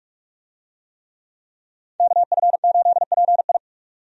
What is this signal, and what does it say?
Morse code, heard as a keyed tone, about 46 wpm, 700 Hz. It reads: KF8PI
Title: USA Call Signs - 5 Characters - 45wpm